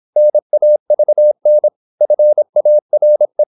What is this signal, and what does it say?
Morse code, heard as a keyed tone, about 26 wpm, 600 Hz. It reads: NAVN FARE
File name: Korte ord DK 26wpm